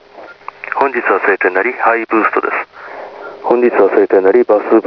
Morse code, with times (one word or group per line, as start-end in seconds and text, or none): none